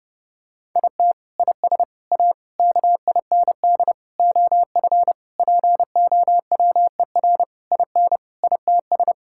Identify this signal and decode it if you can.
Morse code, heard as a keyed tone, about 30 wpm, 700 Hz. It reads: IT IS A KIND OF POWER IN ITS
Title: Sentences from Top 1000 Words - Review Part 5 - 30wpm